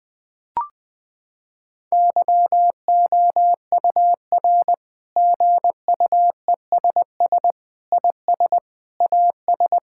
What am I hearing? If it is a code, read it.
Morse code, heard as a keyed tone, about 20 wpm, 700 Hz. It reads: YOUR GUESS IS AS
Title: Sentences with 7-Word Phrases - 20wpm